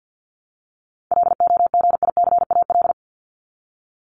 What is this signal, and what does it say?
Morse code, heard as a keyed tone, about 48 wpm, 700 Hz. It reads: LOGICAL
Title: Top 1501-1750 Words 50wpm